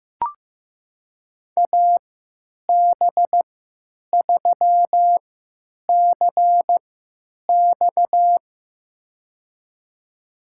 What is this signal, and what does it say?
Morse code, heard as a keyed tone, about 15 wpm, spaced as for 9 wpm, 700 Hz. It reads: AB3CX
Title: USA Call Signs - 5 Characters - 3x Character Spacing - 15wpm